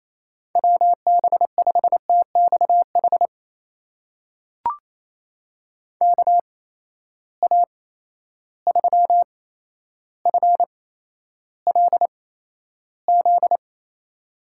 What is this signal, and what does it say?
Morse code, heard as a keyed tone, about 28 wpm, spaced as for 12 wpm, 700 Hz. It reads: WB5TXH KA3FLZ
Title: USA Call Signs - 6 Characters - 8x Character Spacing - 28wpm